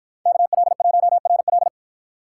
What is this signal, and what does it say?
Morse code, heard as a keyed tone, about 53 wpm, 700 Hz. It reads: KL1RL